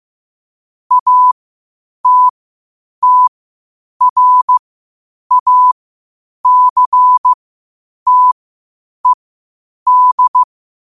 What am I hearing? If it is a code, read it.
Morse code, heard as a keyed tone, about 15 wpm, spaced as for 8 wpm, 1000 Hz. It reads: ATTRACTED
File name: qotdfiles / 8wpm